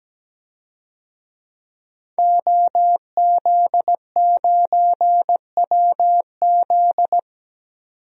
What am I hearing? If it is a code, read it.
Morse code, heard as a keyed tone, about 17 wpm, 700 Hz. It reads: OZ9WZ
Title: DXCC-100 Call Signs - 5 Characters - 17wpm